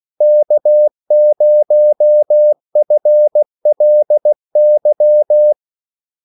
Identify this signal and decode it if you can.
Morse code, heard as a keyed tone, about 16 wpm, 600 Hz. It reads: K0FLY
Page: Kaldesignaler 16 wpm